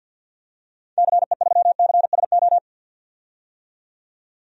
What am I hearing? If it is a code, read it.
Morse code, heard as a keyed tone, about 52 wpm, 700 Hz. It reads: CE3CSO
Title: DXCC-100 Call Signs - 6 Characters - 50wpm